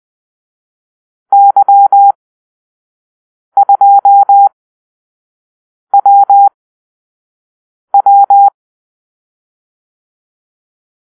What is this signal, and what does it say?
Morse code, heard as a keyed tone, about 20 wpm, spaced as for 5 wpm, 800 Hz. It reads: Y2WW
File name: arrl / Call-Sign Drill-4